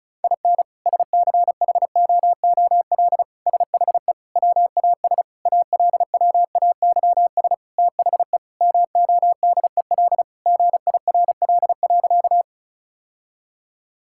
Morse code, heard as a keyed tone, about 34 wpm, 700 Hz. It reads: IN SCHOOL SHE WAS ALWAYS THE MODEL GIRL.
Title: Sentences from Top 1000 Words - Review Part 5 - 35wpm